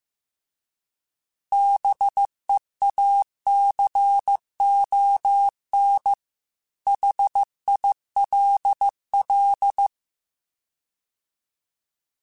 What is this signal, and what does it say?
Morse code, heard as a keyed tone, about 15 wpm, 800 Hz. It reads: BEACON HILL
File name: fromSMs / FromSMs mp3s / Cities@13